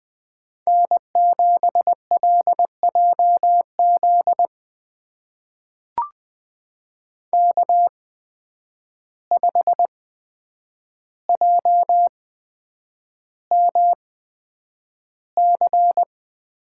Morse code, heard as a keyed tone, about 20 wpm, spaced as for 9 wpm, 700 Hz. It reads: N7LJZ K5JMC
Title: USA Call Signs - 5 Characters - 8x Character Spacing - 20wpm